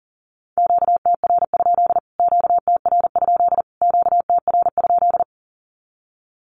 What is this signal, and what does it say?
Morse code, heard as a keyed tone, about 40 wpm, 700 Hz. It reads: QTR? QTR? QTR?
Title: Q-Codes Repeated 3x 40wpm